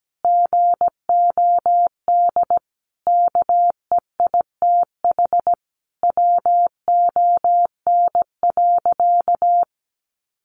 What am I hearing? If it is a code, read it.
Morse code, heard as a keyed tone, about 17 wpm, 700 Hz. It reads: GOD KEITH WON.
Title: A Sound Of Thunder 17wpm